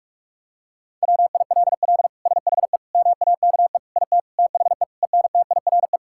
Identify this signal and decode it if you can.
Morse code, heard as a keyed tone, about 46 wpm, 700 Hz. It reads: WILL SHE MAKE IT THE ENTIRE
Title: Sentences from Top 1000 Words - Review Part 5 - 45wpm